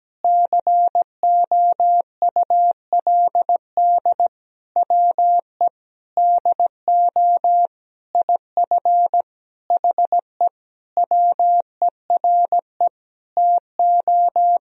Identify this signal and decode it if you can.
Morse code, heard as a keyed tone, about 17 wpm, 700 Hz. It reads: COULD WE DO IF HE WERE TO
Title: Sentences from Top 400 Words and Abbreviations 17wpm